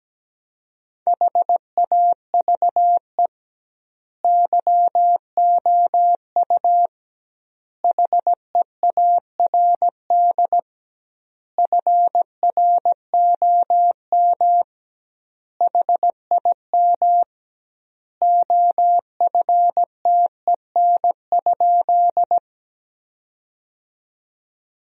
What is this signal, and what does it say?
Morse code, heard as a keyed tone, about 17 wpm, 700 Hz. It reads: HAVE YOU HEARD FROM HIM OFTEN?
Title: 2x Word-Spacing - Sentences from Top 400 Words - 17wpm